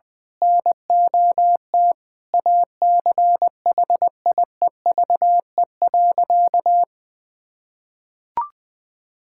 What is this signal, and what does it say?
Morse code, heard as a keyed tone, about 20 wpm, 700 Hz. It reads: NOT ACHIEVE.
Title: Sentences from Top 1000 Words - Review Part 6 - 20wpm